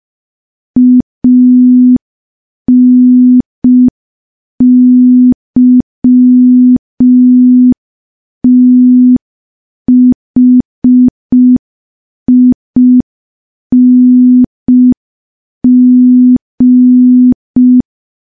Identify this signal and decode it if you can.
Morse code, heard as a keyed tone, about 5 wpm, 250 Hz. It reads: ANYTHING
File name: morse / morse-training-05WPM